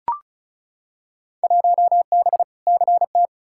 Morse code, heard as a keyed tone, about 36 wpm, 700 Hz. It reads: 1B CT
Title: Field Day Exchanges Encore - 35wpm